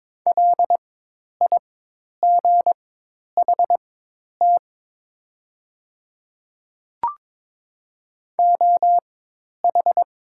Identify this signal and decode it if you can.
Morse code, heard as a keyed tone, about 22 wpm, spaced as for 10 wpm, 700 Hz. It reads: LIGHT OH
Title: 4x Character Spacing - Top 500 Words - 22wpm